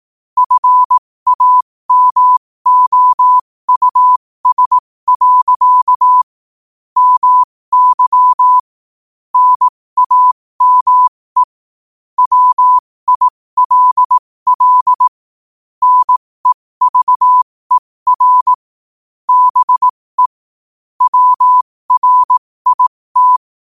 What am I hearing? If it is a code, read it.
Morse code, heard as a keyed tone, about 18 wpm, spaced as for 15 wpm, 1000 Hz. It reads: FAMOUS. MY NAME WILL NEVER BE WRIT
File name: qotdfiles / 15wpm